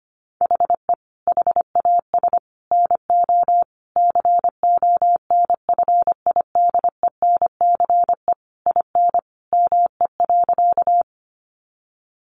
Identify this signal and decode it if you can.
Morse code, heard as a keyed tone, about 25 wpm, 700 Hz. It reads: HE HAS NO CONFIDENCE IN ME.
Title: Sentences from Top 1500 Words 25wpm